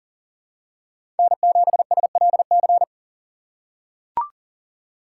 Morse code, heard as a keyed tone, about 40 wpm, 700 Hz. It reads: N7SLC
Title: USA Call Signs - 5 Characters - 6x Character Spacing - 40wpm